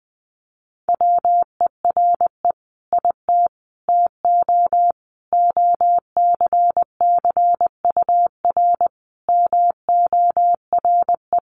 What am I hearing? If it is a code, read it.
Morse code, heard as a keyed tone, about 20 wpm, 700 Hz. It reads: WERE IT TO OCCUR MORE
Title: Sentences from Top 1300 Words 20wpm